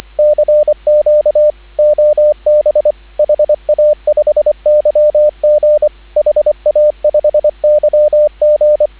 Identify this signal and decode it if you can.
Morse code, heard as a keyed tone, about 24 wpm, 590 Hz, with radio noise. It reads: CQ OB HA5YG HA5YG